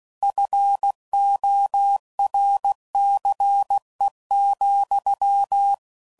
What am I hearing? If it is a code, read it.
Morse code, heard as a keyed tone, about 16 wpm, 800 Hz. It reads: FORCE,
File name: CW Code Groups @15WPM / Technical Subjects / XL